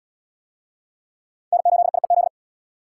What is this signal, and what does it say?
Morse code, heard as a keyed tone, about 72 wpm, 700 Hz. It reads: N8IEZ